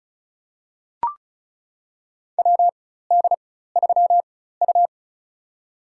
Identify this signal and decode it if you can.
Morse code, heard as a keyed tone, about 36 wpm, spaced as for 16 wpm, 700 Hz. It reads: WD3U
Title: USA Call Signs - 4 Characters - 4x Character Spacing - 35wpm